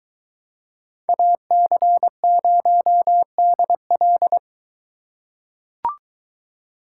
Morse code, heard as a keyed tone, about 21 wpm, 700 Hz. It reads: AC0DL